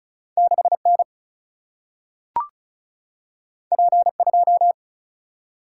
Morse code, heard as a keyed tone, about 36 wpm, 700 Hz. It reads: 6 N P 2
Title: Single-Letter Single-Number 35wpm